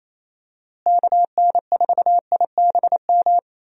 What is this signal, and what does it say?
Morse code, heard as a keyed tone, about 28 wpm, 700 Hz. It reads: KN4IBM
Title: USA Call Signs - 6 Characters - Repeated 2x - 28wpm